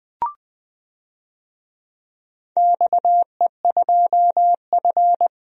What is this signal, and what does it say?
Morse code, heard as a keyed tone, about 20 wpm, 700 Hz. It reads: XE2F
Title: DXCC-100 Call Signs - 4 Characters - 20wpm